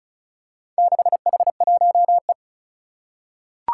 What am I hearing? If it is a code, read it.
Morse code, heard as a keyed tone, about 36 wpm, 700 Hz. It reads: 6H1E
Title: Four Challenging Letter-Number Combinations 35wpm